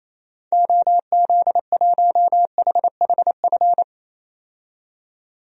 Morse code, heard as a keyed tone, about 28 wpm, 700 Hz. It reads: OZ1HHF
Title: DXCC-100 Call Signs - 6 Characters - 28wpm